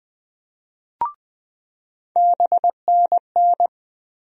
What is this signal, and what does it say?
Morse code, heard as a keyed tone, about 20 wpm, 700 Hz. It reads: BNN
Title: USA Call Sign Postfixes - 3 Letters 20wpm